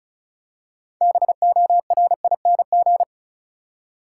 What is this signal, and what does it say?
Morse code, heard as a keyed tone, about 36 wpm, 700 Hz. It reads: BORING
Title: Six Letter Words - 35wpm